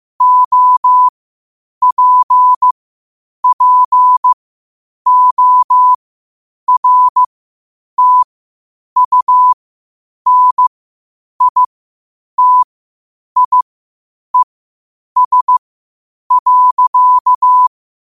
Morse code, heard as a keyed tone, about 15 wpm, spaced as for 8 wpm, 1000 Hz. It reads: OPPORTUNITIES.
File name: qotdfiles / 8wpm